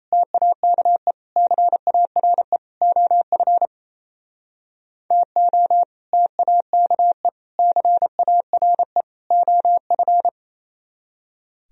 Morse code, heard as a keyed tone, about 38 wpm, 700 Hz. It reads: TAKE CARE OF TO TAKE CARE OF